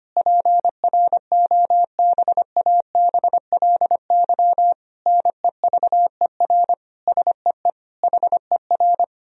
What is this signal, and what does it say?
Morse code, heard as a keyed tone, about 25 wpm, 700 Hz. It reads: PROBABLY NEVER SEE HER
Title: Sentences from Top 1000 Words - Review Part 4 - 25wpm